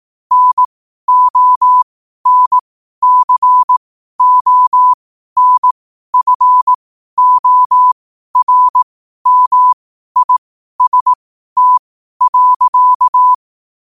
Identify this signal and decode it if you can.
Morse code, heard as a keyed tone, about 18 wpm, spaced as for 12 wpm, 1000 Hz. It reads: NONCONFORMIST.
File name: qotdfiles / 12wpm